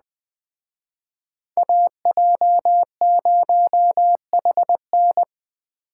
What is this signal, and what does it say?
Morse code, heard as a keyed tone, about 21 wpm, 700 Hz. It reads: AJ0HN
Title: Speed Racing - USA 2x2 Call Signs 17wpm